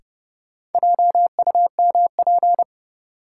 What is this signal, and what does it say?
Morse code, heard as a keyed tone, about 30 wpm, 700 Hz. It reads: JUMP
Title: Sentences from Top 1300 Words - New Word Drill - 30wpm